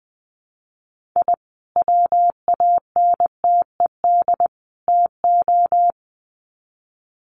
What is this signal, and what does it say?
Morse code, heard as a keyed tone, about 20 wpm, 700 Hz. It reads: I WANTED TO
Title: Sets of 3 Words 20wpm